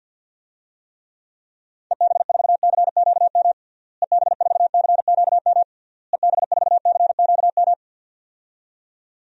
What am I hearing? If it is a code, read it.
Morse code, heard as a keyed tone, about 48 wpm, 700 Hz. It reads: EB4CQM EB4CQM EB4CQM
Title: DXCC-100 Call Signs - 6 Characters - Repeated 3x - 50wpm